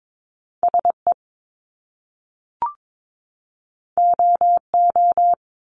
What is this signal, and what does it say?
Morse code, heard as a keyed tone, about 22 wpm, 700 Hz. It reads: SE OO
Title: Top Two Letter Combinations in Words 22wpm